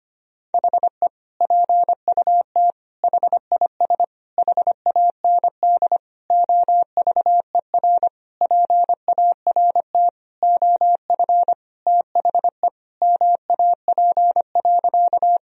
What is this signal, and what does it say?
Morse code, heard as a keyed tone, about 25 wpm, 700 Hz. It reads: HE PUT HIS HAND OVER PART OF THE MAP.
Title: Sentences from Top 1000 Words - Review Part 2 - 25wpm